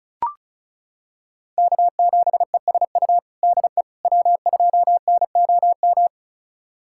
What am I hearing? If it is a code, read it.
Morse code, heard as a keyed tone, about 34 wpm, 700 Hz. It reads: K7ESU DE W2NOM
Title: USA Callsign DE Callsign - 5 Characters - 35wpm